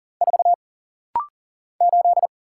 Morse code, heard as a keyed tone, about 40 wpm, 700 Hz. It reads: 4 8
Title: Single Letter-Number - Rapid-Fire 40wpm